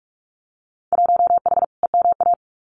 Morse code, heard as a keyed tone, about 46 wpm, 700 Hz. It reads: 1H EMA
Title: Winter Field Day Exchanges 45wpm